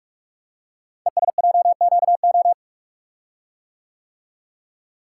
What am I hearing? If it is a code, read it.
Morse code, heard as a keyed tone, about 44 wpm, 700 Hz. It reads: ESJQO
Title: Code Groups - 5 Letters - 45wpm